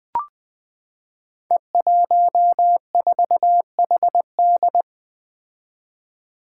Morse code, heard as a keyed tone, about 21 wpm, 700 Hz. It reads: E14HD
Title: Five Challenging Letter-Number Combinations 20wpm